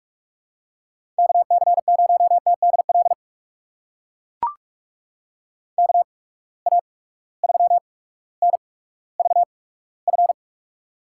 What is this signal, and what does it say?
Morse code, heard as a keyed tone, about 42 wpm, spaced as for 19 wpm, 700 Hz. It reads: KC0TDL KA3NVF
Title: USA Call Signs - 6 Characters - 8x Character Spacing - 45wpm